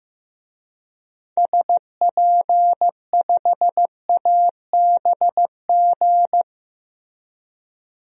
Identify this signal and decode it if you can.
Morse code, heard as a keyed tone, about 15 wpm, 700 Hz. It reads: SP5ABG
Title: DXCC-100 Call Signs - 6 Characters - Repeated 2x - 15wpm